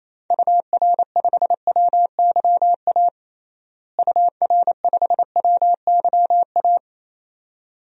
Morse code, heard as a keyed tone, about 28 wpm, 700 Hz. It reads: UR5WYA UR5WYA
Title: DXCC-100 Call Signs - 6 Characters - Repeated 2x - 28wpm